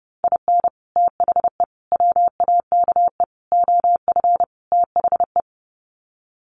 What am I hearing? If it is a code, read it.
Morse code, heard as a keyed tone, about 28 wpm, 700 Hz. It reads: IN THE WAKE OF THE
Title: Sets of 5 Words 30wpm